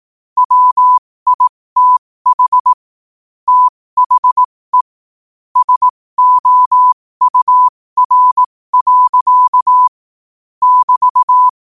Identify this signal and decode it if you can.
Morse code, heard as a keyed tone, about 18 wpm, spaced as for 15 wpm, 1000 Hz. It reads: WITH THE SOUR. <BT>
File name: qotdfiles / 15wpm